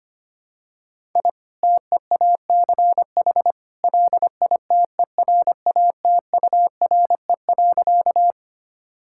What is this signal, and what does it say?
Morse code, heard as a keyed tone, about 25 wpm, 700 Hz. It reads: I TEACH LITERATURE.
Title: Sentences from Top 1000 Words - Review Part 6 - 25wpm